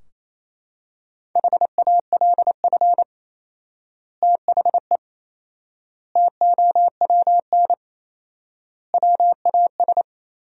Morse code, heard as a keyed tone, about 28 wpm, 700 Hz. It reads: HALF THE TOWN WAS
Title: 4x Word-Spacing - Sentences from Top 500 Words - 28wpm